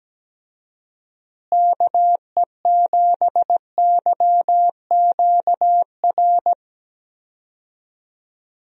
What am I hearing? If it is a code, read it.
Morse code, heard as a keyed tone, about 17 wpm, 700 Hz. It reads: KE7YQR
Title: USA Call Signs - 6 Characters - Repeated 2x - 17wpm